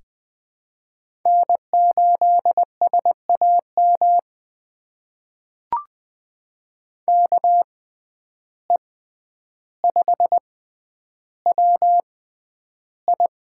Morse code, heard as a keyed tone, about 21 wpm, spaced as for 10 wpm, 700 Hz. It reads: N8SAM KE5WI
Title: USA Call Signs - 5 Characters - 6x Character Spacing - 20wpm